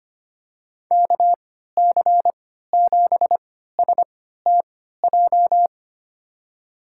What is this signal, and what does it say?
Morse code, heard as a keyed tone, about 25 wpm, spaced as for 14 wpm, 700 Hz. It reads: KC7STJ
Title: USA Call Signs - 6 Characters - 3x Character Spacing - 25wpm